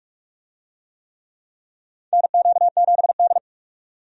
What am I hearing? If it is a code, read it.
Morse code, heard as a keyed tone, about 46 wpm, 700 Hz. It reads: NQ7D
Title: USA Call Signs - 4 Characters - 45wpm